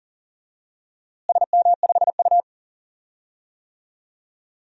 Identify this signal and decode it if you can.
Morse code, heard as a keyed tone, about 37 wpm, 700 Hz. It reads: SM5U
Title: DXCC-100 Call Signs - 4 Characters - Repeated 3x - 40wpm